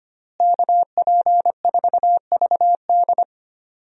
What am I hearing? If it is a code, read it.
Morse code, heard as a keyed tone, about 25 wpm, 700 Hz. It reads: KP4VD
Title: DXCC-100 Call Signs - 5 Characters - Repeated 3x - 25wpm